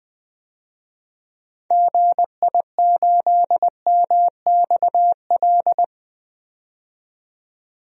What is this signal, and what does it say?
Morse code, heard as a keyed tone, about 20 wpm, 700 Hz. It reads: GI8MXL
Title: DXCC-100 Call Signs - 6 Characters - 20wpm